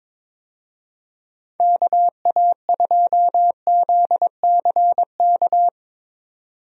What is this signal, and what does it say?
Morse code, heard as a keyed tone, about 22 wpm, 700 Hz. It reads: KA2ZCK
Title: USA Call Signs - 6 Characters - 22wpm